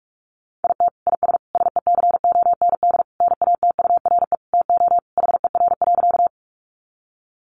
Morse code, heard as a keyed tone, about 44 wpm, 700 Hz. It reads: IT IS SECOND NATURE TO HER.
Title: Sentences from Top 600 Words 45wpm